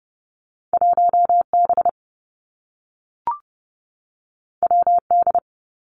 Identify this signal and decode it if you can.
Morse code, heard as a keyed tone, about 30 wpm, 700 Hz. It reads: 1 B W D
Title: Two Challenging Letter-Number Combinations 30wpm